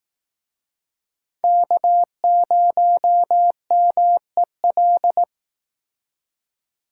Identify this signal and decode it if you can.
Morse code, heard as a keyed tone, about 19 wpm, 700 Hz. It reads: K0MEL
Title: Speed Racing - USA 1x3 Call Signs 15wpm